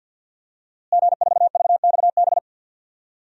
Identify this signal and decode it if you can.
Morse code, heard as a keyed tone, about 52 wpm, 700 Hz. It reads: G4VXB
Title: DXCC-100 Call Signs - 5 Characters - Repeated 2x - 50wpm